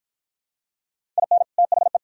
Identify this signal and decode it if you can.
Morse code, heard as a keyed tone, about 52 wpm, 700 Hz. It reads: IN THE